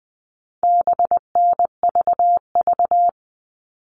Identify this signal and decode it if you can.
Morse code, heard as a keyed tone, about 20 wpm, 700 Hz. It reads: BNVV
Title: All Cut Numbers - 4-Digit 20wpm